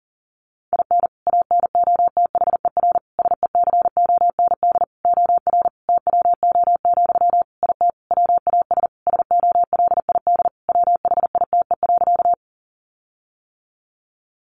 Morse code, heard as a keyed tone, about 40 wpm, 700 Hz. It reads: IN ANOTHER SECOND OR TWO, IT WAS SOLID WHITE.
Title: Sentences from Top 1500 Words 40wpm